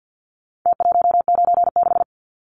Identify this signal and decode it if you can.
Morse code, heard as a keyed tone, about 50 wpm, 700 Hz. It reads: T196
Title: Numbers - Contest Style - 4-Digit 50wpm